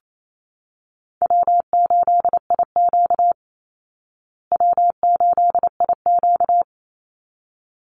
Morse code, heard as a keyed tone, about 28 wpm, 700 Hz. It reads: W8IQ W8IQ
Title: Easy Call Signs Repeated 2x - 28wpm